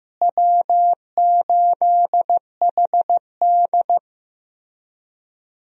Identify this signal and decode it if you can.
Morse code, heard as a keyed tone, about 15 wpm, 700 Hz. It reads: W8HD
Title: USA Call Signs - 4 Characters - Repeated 3x - 15wpm